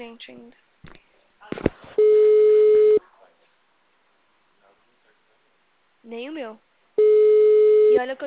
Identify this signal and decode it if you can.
Ringback tone (Europe, 425 Hz), with speech.
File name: out- -2025- - - .204